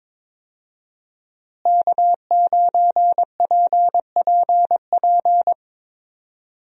Morse code, heard as a keyed tone, about 22 wpm, 700 Hz. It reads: K9PPP
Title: USA Call Signs - 5 Characters - Repeated 3x - 22wpm